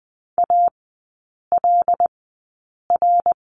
Morse code, heard as a keyed tone, about 20 wpm, spaced as for 8 wpm, 700 Hz. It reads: ALR
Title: 172 Lesson 28e - Number 9 - Three Characters 20wpm